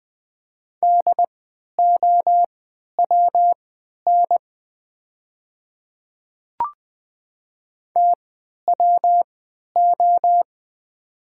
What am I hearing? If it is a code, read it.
Morse code, heard as a keyed tone, about 20 wpm, spaced as for 11 wpm, 700 Hz. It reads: DOWN TWO
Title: 3x Character Spacing - Top 100 Words - 20wpm